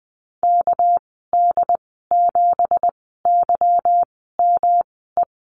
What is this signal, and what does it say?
Morse code, heard as a keyed tone, about 20 wpm, 700 Hz. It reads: K D 7 Y M E
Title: 231 Lesson 37g - Letter x - Call Signs 20wpm